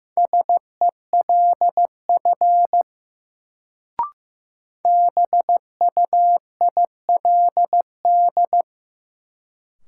Morse code, heard as a keyed tone, about 15 wpm, 700 Hz. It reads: SELF BUILD
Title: Sentences from Top 1000 Words - Review Part 1 - New Word Drill - 15wpm